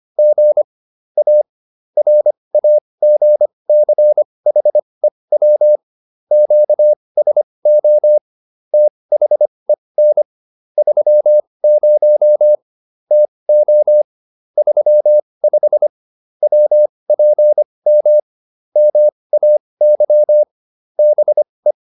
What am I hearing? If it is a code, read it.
Morse code, heard as a keyed tone, about 25 wpm, spaced as for 20 wpm, 600 Hz. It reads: G A RAGCHEW QSO THEN 30 TO 35 WPM MAY BE